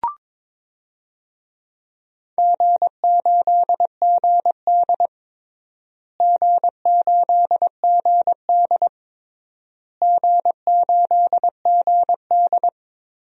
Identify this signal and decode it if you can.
Morse code, heard as a keyed tone, about 22 wpm, 700 Hz. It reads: G8GD G8GD G8GD
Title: DXCC-100 Call Signs - 4 Characters - Repeated 3x - 22wpm